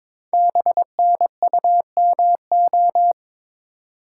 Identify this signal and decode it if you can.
Morse code, heard as a keyed tone, about 22 wpm, 700 Hz. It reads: BNUMO
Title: Code Groups - 5 Letters - 22wpm